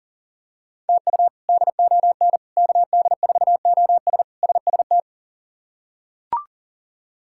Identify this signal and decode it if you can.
Morse code, heard as a keyed tone, about 40 wpm, 700 Hz. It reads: TU DON KD4OS SST
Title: K1USN SST QSO Practice - 40wpm